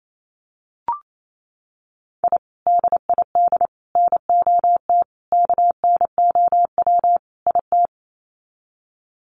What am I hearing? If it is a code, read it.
Morse code, heard as a keyed tone, about 28 wpm, 700 Hz. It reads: I DID NOT KNOW IT
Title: Sets of 5 Words Encore2 28wpm